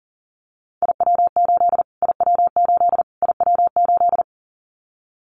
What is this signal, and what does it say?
Morse code, heard as a keyed tone, about 40 wpm, 700 Hz. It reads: IW8 IW8 IW8
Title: Callsign Prefixes for Top 25 DXCC - Repeated 3x - 40wpm